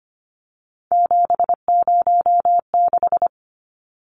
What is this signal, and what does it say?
Morse code, heard as a keyed tone, about 25 wpm, 700 Hz. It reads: 706
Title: Numbers - 3-Digit 25wpm